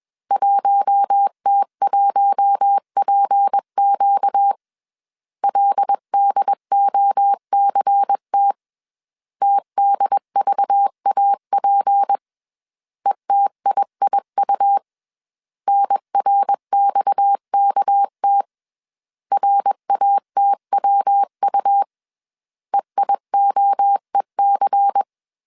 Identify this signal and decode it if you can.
Morse code, heard as a keyed tone, about 21 wpm, 800 Hz. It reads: 1T1PQ LDOCT TDVAP ETIIU NRXKT RATWU EIOEC
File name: CW / DET / 1 6 1500C S B 20wpm 2048